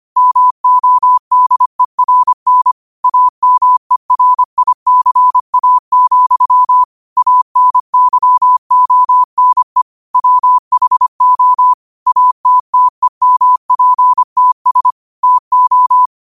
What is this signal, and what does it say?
Morse code, heard as a keyed tone, about 25 wpm, 1000 Hz. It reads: MODERN AMERICA, ANYONE WHO ATTEMPTS TO
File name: qotdfiles / 25wpm